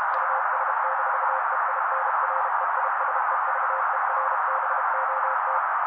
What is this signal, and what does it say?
Morse code, heard as a keyed tone, about 32 wpm, 530 Hz. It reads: US-EINREISEVERBOT
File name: daily / 31wpm noise